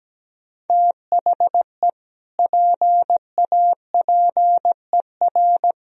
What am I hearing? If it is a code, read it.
Morse code, heard as a keyed tone, about 17 wpm, 700 Hz. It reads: THE PAPER
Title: Sentences from Top 1000 Words - Review Part 1 - 17wpm